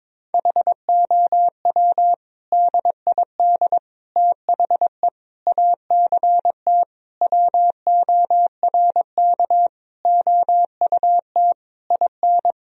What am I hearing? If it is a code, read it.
Morse code, heard as a keyed tone, about 21 wpm, 700 Hz. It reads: HOW DID THE ACT WORK OUT IN
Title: Sentences from Top 1000 Words - Review Part 4 - 22wpm